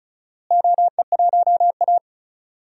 Morse code, heard as a keyed tone, about 34 wpm, 700 Hz. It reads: OE1A
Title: DXCC-100 Call Signs - 4 Characters - Repeated 2x - 35wpm